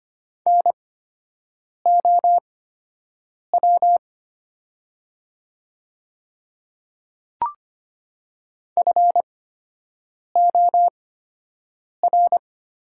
Morse code, heard as a keyed tone, about 25 wpm, spaced as for 7 wpm, 700 Hz. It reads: NOW FOR
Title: 8x Character Spacing - Top 100 Words - 25wpm